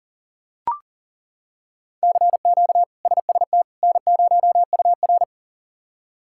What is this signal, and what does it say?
Morse code, heard as a keyed tone, about 38 wpm, 700 Hz. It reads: CQ SST N0UR
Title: K1USN SST QSO Practice - 40wpm